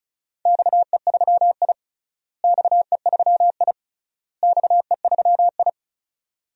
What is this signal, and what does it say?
Morse code, heard as a keyed tone, about 36 wpm, 700 Hz. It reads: XE3I XE3I XE3I
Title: DXCC-100 Call Signs - 4 Characters - Repeated 3x - 35wpm